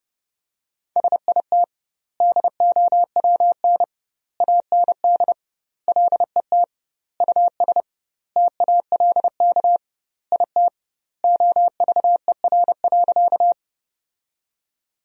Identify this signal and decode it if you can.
Morse code, heard as a keyed tone, about 32 wpm, 700 Hz. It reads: SIT DOWN AND LET US TALK IT OVER.
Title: 2x Word-Spacing - Sentences from Top 500 Words - 30wpm